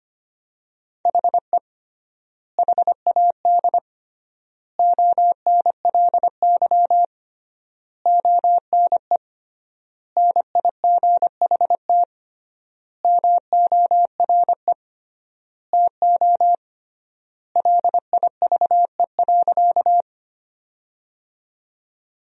Morse code, heard as a keyed tone, about 25 wpm, 700 Hz. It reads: HE HAD ONLY ONE NIGHT MORE TO LIVE.
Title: 3x Word-Spacing - Sentences from Top 300 Words - 25wpm